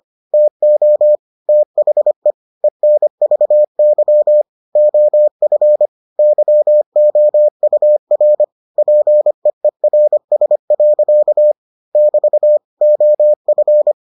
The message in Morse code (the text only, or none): TO THE ENVY OF YOUR PEERS. <BT> OF